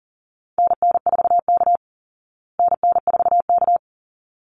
Morse code, heard as a keyed tone, about 40 wpm, 700 Hz. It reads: NN4K NN4K
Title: Call Signs Repeated 3x - 40wpm